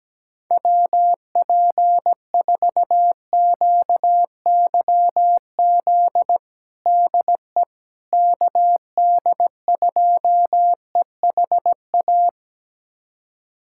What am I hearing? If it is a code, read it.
Morse code, heard as a keyed tone, about 17 wpm, 700 Hz. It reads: WP4QYZ DE KD2EHA
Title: USA Callsign DE Callsign - 6 Characters - 17wpm